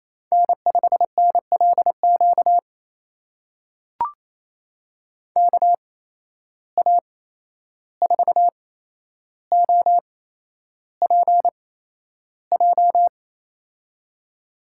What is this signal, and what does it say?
Morse code, heard as a keyed tone, about 28 wpm, spaced as for 11 wpm, 700 Hz. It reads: N5NLQ KA4OPJ
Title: USA Call Signs - 8x Character Spacing - 28wpm